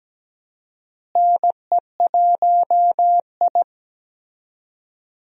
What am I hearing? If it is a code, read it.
Morse code, heard as a keyed tone, about 17 wpm, 700 Hz. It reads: NE1I
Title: USA Call Signs - 4 Characters - Repeated 2x - 17wpm